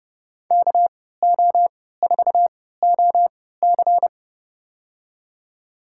Morse code, heard as a keyed tone, about 32 wpm, spaced as for 17 wpm, 700 Hz. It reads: KO4OC
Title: USA Call Signs - 5 Characters - 3x Character Spacing - 30wpm